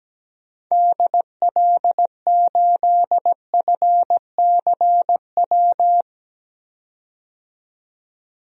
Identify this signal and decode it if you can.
Morse code, heard as a keyed tone, about 17 wpm, 700 Hz. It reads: DL8FCW
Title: DXCC-100 Call Signs - 6 Characters - 17wpm